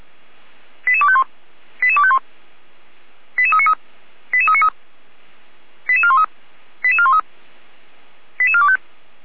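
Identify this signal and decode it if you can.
Five-tone selective call (ZVEI), heard as quick runs of five tones, tones of about 70 ms. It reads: 88251 88251 88393 88393 88413 88413 88425
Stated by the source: ZVEI five-tone selective call received on an SDR.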